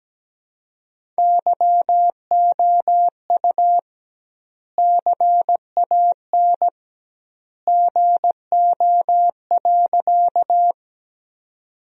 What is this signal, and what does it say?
Morse code, heard as a keyed tone, about 17 wpm, 700 Hz. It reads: YOU CAN GO.
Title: 2x Word-Spacing - Sentences from Top 100 Words - 17wpm